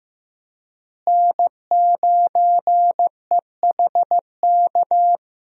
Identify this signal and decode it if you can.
Morse code, heard as a keyed tone, about 16 wpm, 700 Hz. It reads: N9EHK
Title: USA Call Signs - 5 Characters - Repeated 3x - 15wpm